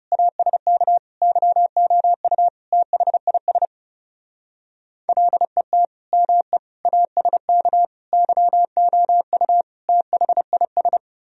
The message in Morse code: ASK YOU THIS LET ME ASK YOU THIS

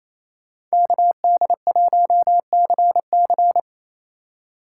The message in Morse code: KD1CC

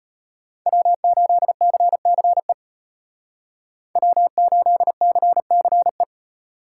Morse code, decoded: W8CCE W8CCE

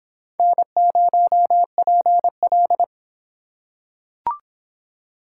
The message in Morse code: N0PL

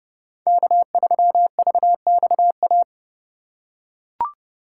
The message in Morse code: K3VXA

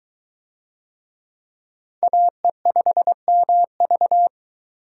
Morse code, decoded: AE5MV

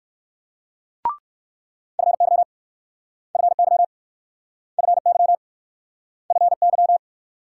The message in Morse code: FY FY FY FY

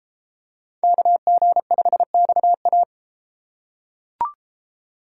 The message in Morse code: KG5XA

